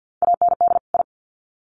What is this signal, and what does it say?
Morse code, heard as a keyed tone, about 50 wpm, 700 Hz. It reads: AND I
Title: Sets of 2 Words 50wpm